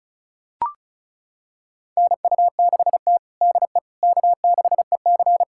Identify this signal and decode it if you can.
Morse code, heard as a keyed tone, about 36 wpm, 700 Hz. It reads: NU6T DE K6EC
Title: USA Callsign DE Callsign - 4 Characters - 35wpm